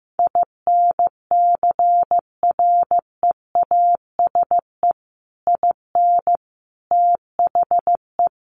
Morse code, heard as a keyed tone, about 15 wpm, 700 Hz. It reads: INCREASE IN THE
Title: Sets of 5 Words 15wpm